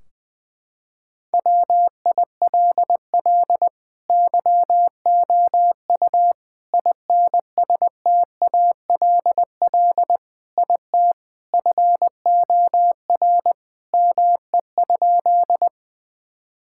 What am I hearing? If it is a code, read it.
Morse code, heard as a keyed tone, about 20 wpm, 700 Hz. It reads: WILL YOU INSTALL IT FOR ME?
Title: Sentences from Top 1600 Words 20wpm